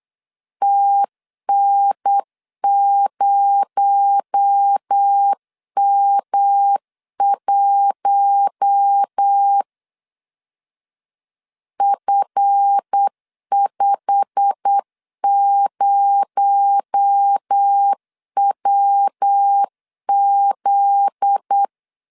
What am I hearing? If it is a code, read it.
Morse code, heard as a keyed tone, about 8 wpm, 800 Hz. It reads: TN0M1 F50WZ
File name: LEZIONI / 8 YJ45 / 8 YJ45 8wpm 3214